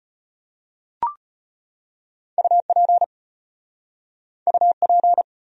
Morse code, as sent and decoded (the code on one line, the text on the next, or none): ..- .--. / ..- .--.
UP UP